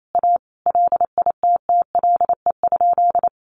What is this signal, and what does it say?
Morse code, heard as a keyed tone, about 28 wpm, 700 Hz. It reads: A LITTLE?
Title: Sentences from Top 500 Words 28wpm